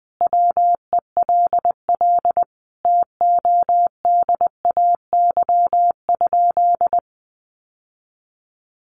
Morse code, heard as a keyed tone, about 20 wpm, 700 Hz. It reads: WELL TODAY?
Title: Sentences from Top 1200 Words 20wpm